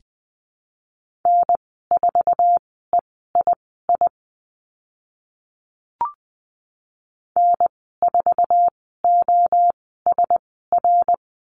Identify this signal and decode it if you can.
Morse code, heard as a keyed tone, about 20 wpm, spaced as for 15 wpm, 700 Hz. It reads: N4EII N4OSR
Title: 055 Lesson 10g - Letter r - Call Signs 20wpm